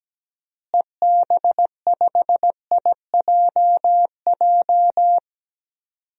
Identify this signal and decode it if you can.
Morse code, heard as a keyed tone, about 17 wpm, 700 Hz. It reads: EB5IJJ
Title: DXCC-100 Call Signs - 6 Characters - Repeated 2x - 17wpm